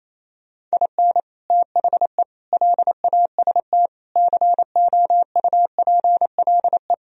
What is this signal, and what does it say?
Morse code, heard as a keyed tone, about 35 wpm, 700 Hz. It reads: IN THE LAST COUPLE